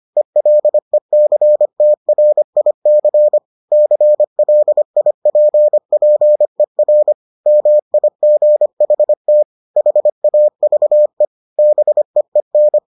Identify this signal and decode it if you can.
Morse code, heard as a keyed tone, about 25 wpm, 600 Hz. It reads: ELECTRIC CLIPPER MIGHT HAVE BEEN